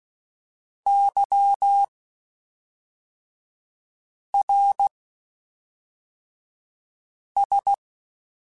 Morse code, heard as a keyed tone, about 16 wpm, spaced as for 3 wpm, 800 Hz. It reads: YRS
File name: (spaced) / QSOs Spaced-mp3s / WA7QOX 2 @5 sp